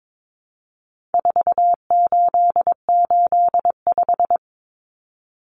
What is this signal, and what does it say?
Morse code, heard as a keyed tone, about 22 wpm, 700 Hz. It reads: 4885
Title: Common Cut Numbers - 4-Digit 22wpm